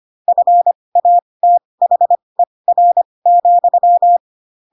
Morse code, heard as a keyed tone, about 25 wpm, spaced as for 20 wpm, 700 Hz. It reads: FATHER,
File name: Pinocchio / Pinocchio ch8 15 25 ik1hge